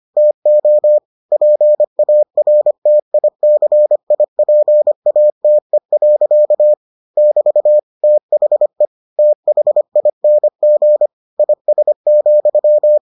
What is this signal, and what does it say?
Morse code, heard as a keyed tone, about 25 wpm, 600 Hz. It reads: TO PARTICIPATE. <BT> THE THING IS,